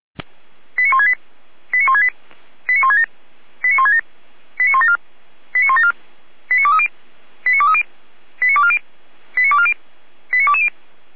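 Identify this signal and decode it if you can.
Five-tone selective call (ZVEI), heard as quick runs of five tones, tones of about 70 ms. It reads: 89168 89168 89167 89167 89174 89174 89230 89230 89240 89240 89220
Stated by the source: ZVEI five-tone selective call received on SDR.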